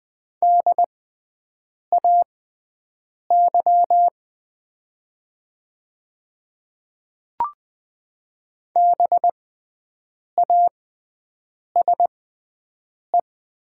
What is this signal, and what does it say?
Morse code, heard as a keyed tone, about 20 wpm, spaced as for 7 wpm, 700 Hz. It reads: DAY BASE
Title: 6x Character Spacing - Top 300 Words - 20wpm